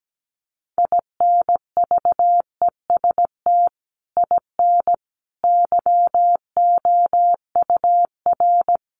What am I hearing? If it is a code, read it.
Morse code, heard as a keyed tone, about 17 wpm, 700 Hz. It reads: INVEST IN YOUR
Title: Sentences from Top 1100 Words 17wpm